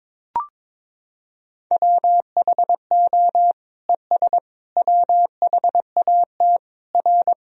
Morse code, heard as a keyed tone, about 22 wpm, 700 Hz. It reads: WHO ES WHAT R
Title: Sentences from Top 100 Words and Abbreviations 22wpm